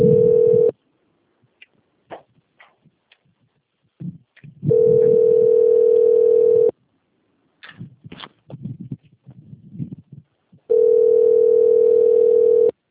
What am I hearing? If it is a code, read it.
Ringback tone (North America), 440 + 480 Hz.